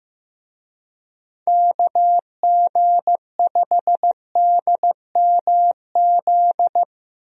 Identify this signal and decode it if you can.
Morse code, heard as a keyed tone, about 15 wpm, 700 Hz. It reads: KG5DMZ
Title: USA Call Signs - 6 Characters - 15wpm